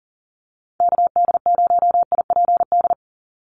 Morse code, heard as a keyed tone, about 40 wpm, 700 Hz. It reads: KD0IPD
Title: USA Call Signs 40wpm